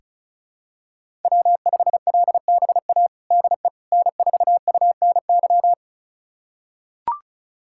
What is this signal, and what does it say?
Morse code, heard as a keyed tone, about 36 wpm, 700 Hz. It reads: W5LBA DE N4UNY
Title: USA Callsign DE Callsign - 5 Characters - 35wpm